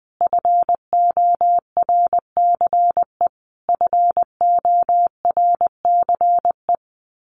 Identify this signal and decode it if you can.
Morse code, heard as a keyed tone, about 20 wpm, 700 Hz. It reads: FORCE FORCE
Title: Top 500 Words Repeated 20wpm